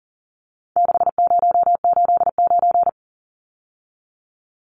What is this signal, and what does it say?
Morse code, heard as a keyed tone, about 40 wpm, 700 Hz. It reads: 6089
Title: Numbers - 4-Digit 40wpm